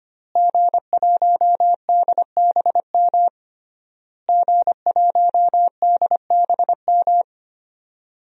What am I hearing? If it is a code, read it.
Morse code, heard as a keyed tone, about 25 wpm, 700 Hz. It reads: G1DBM G1DBM
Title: DXCC-100 Call Signs - 5 Characters - Repeated 3x - 25wpm